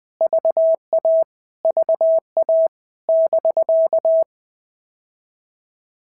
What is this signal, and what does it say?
Morse code, heard as a keyed tone, about 20 wpm, 650 Hz. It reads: VA VA <BK>
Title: POTA QSO Practice - 20wpm